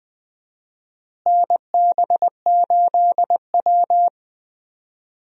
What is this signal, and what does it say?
Morse code, heard as a keyed tone, about 21 wpm, 700 Hz. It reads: NB8W
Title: USA Call Signs - 4 Characters - Repeated 2x - 20wpm